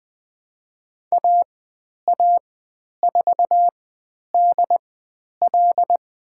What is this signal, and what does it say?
Morse code, heard as a keyed tone, about 20 wpm, spaced as for 10 wpm, 700 Hz. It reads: AA4DL
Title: 095 Lesson 16g - Letter c - Call Signs 20wpm